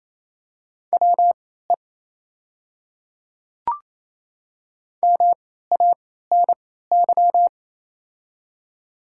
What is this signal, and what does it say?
Morse code, heard as a keyed tone, about 28 wpm, spaced as for 16 wpm, 690 Hz. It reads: WE MANY
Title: 3x Character Spacing - Top 100 Words - 28wpm